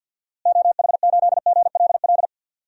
Morse code, heard as a keyed tone, about 53 wpm, 700 Hz. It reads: OH8GLL
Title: DXCC-100 Call Signs - 6 Characters - Repeated 3x - 50wpm